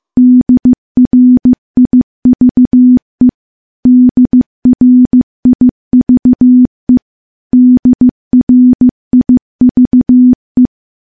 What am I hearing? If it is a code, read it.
Morse code, heard as a keyed tone, about 15 wpm, 250 Hz. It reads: DRIVE DRIVE DRIVE
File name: morse / morse-training-15WPM